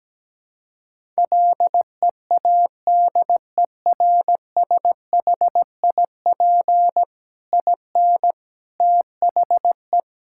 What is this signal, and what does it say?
Morse code, heard as a keyed tone, about 17 wpm, 700 Hz. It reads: LEADERSHIP IN THE
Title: Sentences from Top 1600 Words 17wpm